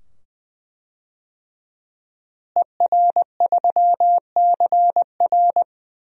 Morse code, heard as a keyed tone, about 20 wpm, 700 Hz. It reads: ER3CR
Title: DXCC-100 Call Signs - 5 Characters - Repeated 2x - 20wpm